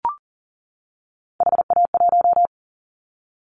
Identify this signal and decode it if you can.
Morse code, heard as a keyed tone, about 39 wpm, 700 Hz. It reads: HA1
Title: Callsign Prefixes for Top 50 DXCC - 40wpm